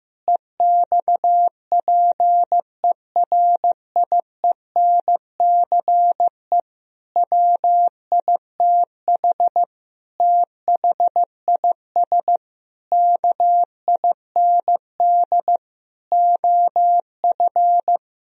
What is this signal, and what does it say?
Morse code, heard as a keyed tone, about 15 wpm, 700 Hz. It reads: EXPERIENCE WITH THIS KIND OF
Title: Sentences from Top 1000 Words - Review Part 3 - 15wpm